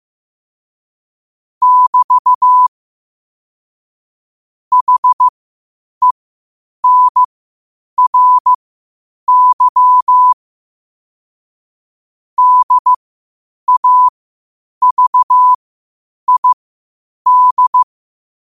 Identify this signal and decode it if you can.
Morse code, heard as a keyed tone, about 15 wpm, spaced as for 8 wpm, 1000 Hz. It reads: <BT> HENRY DAVID
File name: qotdfiles / 8wpm